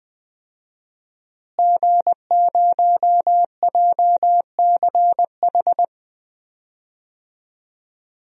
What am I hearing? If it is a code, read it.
Morse code, heard as a keyed tone, about 20 wpm, 700 Hz. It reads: G0JCH
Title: DXCC-100 Call Signs - 5 Characters - Repeated 2x - 20wpm